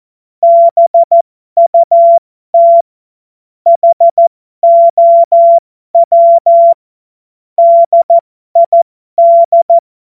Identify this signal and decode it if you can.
Morse code, heard as a keyed tone, about 14 wpm, spaced as for 12 wpm, 680 Hz. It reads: BUT HOW DID
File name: Morsekurs / Witze